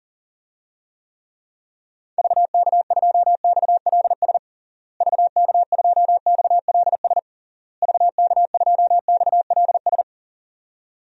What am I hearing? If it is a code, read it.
Morse code, heard as a keyed tone, about 37 wpm, 700 Hz. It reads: VK2XLS VK2XLS VK2XLS
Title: DXCC-100 Call Signs - 6 Characters - Repeated 3x - 40wpm